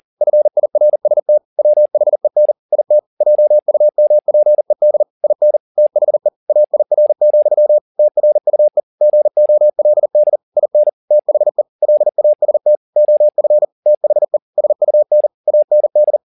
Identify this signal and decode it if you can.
Morse code, heard as a keyed tone, about 40 wpm, 600 Hz. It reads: FIRST WHEN IT JUMPED IN THE AIR, TRUE GOLD IN THE LAST OF THE SUN AND
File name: the old man and the sea 2 40wpm